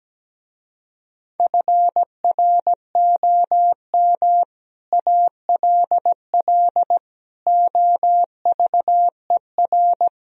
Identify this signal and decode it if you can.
Morse code, heard as a keyed tone, about 17 wpm, 700 Hz. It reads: FROM ALL OVER
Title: Speed-Racing - Sets of 5 Words 17wpm